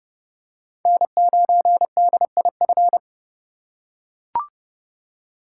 Morse code, heard as a keyed tone, about 30 wpm, 700 Hz. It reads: N9DIF
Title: USA Call Signs - 5 Characters - 4x Character Spacing - 30wpm